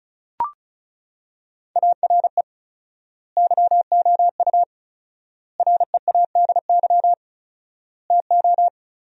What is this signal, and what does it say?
Morse code, heard as a keyed tone, about 34 wpm, 700 Hz. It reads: ARE YOU READY TO
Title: 4x Word-Spacing - Sentences from Top 400 Words - 35wpm